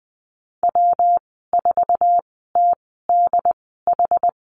Morse code, heard as a keyed tone, about 20 wpm, spaced as for 15 wpm, 700 Hz. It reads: W4TDH
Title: 186 Lesson 30g - Letter b - Call Signs 20wpm